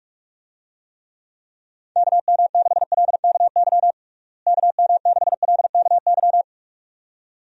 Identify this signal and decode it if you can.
Morse code, heard as a keyed tone, about 46 wpm, 700 Hz. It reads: KM6LKY KM6LKY
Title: USA Call Signs - 6 Characters - Repeated 2x - 45wpm